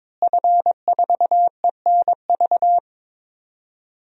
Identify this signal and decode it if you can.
Morse code, heard as a keyed tone, about 22 wpm, 700 Hz. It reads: F4ENV
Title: DXCC-100 Call Signs - 5 Characters - Repeated 2x - 22wpm